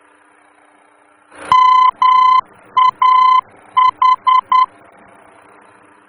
Morse code, heard as a keyed tone, about 10 wpm, 1020 Hz. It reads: MAH
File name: germany / mx / MAH-D-VOR